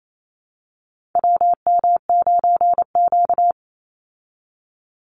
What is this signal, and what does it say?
Morse code, heard as a keyed tone, about 28 wpm, 700 Hz. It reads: WM9Q
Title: Call Signs Repeated 3x - 28wpm